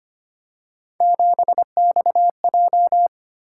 Morse code, heard as a keyed tone, about 25 wpm, 700 Hz. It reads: 7XJ
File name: Three Letter-Number Combinations - 25wpm